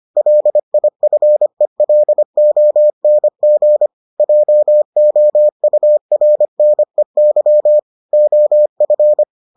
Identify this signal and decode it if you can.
Morse code, heard as a keyed tone, about 25 wpm, 600 Hz. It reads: LIFELONG JOURNEY OF